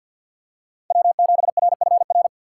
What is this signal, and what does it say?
Morse code, heard as a keyed tone, about 51 wpm, 700 Hz. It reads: W7RFR
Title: USA Call Signs - 5 Characters - 4x Character Spacing - 50wpm